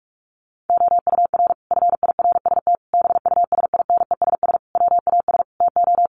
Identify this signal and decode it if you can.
Morse code, heard as a keyed tone, about 44 wpm, 700 Hz. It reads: OUR FIRST BUSINESS WAS TO
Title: Sentences from Top 1300 Words 45wpm